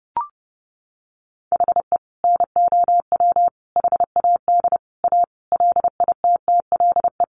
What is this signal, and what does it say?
Morse code, heard as a keyed tone, about 30 wpm, 700 Hz. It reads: HE NOW HAD A LITTLE
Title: Sentences from Top 1500 Words 30wpm